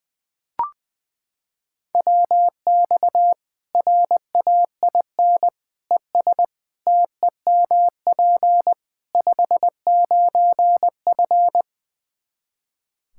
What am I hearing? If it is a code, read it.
Morse code, heard as a keyed tone, about 20 wpm, 700 Hz. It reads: WX RAIN ES TEMP 59F
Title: Practice QSO - Exchange 2 - Info - 20wpm